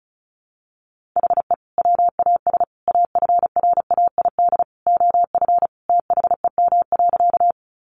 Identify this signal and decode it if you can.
Morse code, heard as a keyed tone, about 34 wpm, 700 Hz. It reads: HE WAS AFRAID OF THEM.
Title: Sentences from Top 1500 Words 35wpm